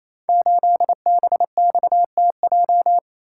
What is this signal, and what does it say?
Morse code, heard as a keyed tone, about 27 wpm, 700 Hz. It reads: 8BXTJ
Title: Code Groups - 5 Letter:Numbers - 28wpm